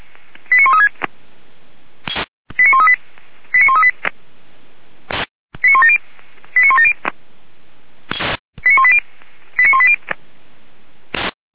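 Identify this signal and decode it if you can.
Five-tone selective call (ZVEI), heard as quick runs of five tones, tones of about 80 ms. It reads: 89136 89138 89138 89170 89170 89180 89180
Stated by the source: ZVEI five-tone selective call, received by SDR.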